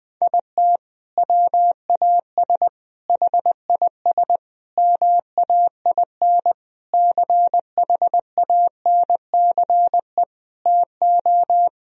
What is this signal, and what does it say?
Morse code, heard as a keyed tone, about 19 wpm, 700 Hz. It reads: IT WAS HIS MAIN CHANCE TO
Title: Sentences from Top 1000 Words - Review Part 2 - 20wpm